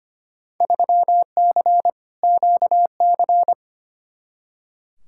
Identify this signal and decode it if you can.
Morse code, heard as a keyed tone, about 25 wpm, 700 Hz. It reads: 3C QC
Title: Field Day Exchanges Encore - 25wpm